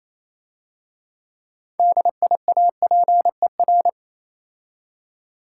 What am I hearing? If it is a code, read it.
Morse code, heard as a keyed tone, about 27 wpm, 700 Hz. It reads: DIAPER
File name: Easy Words - Speed-Racing with Hints - 22wpm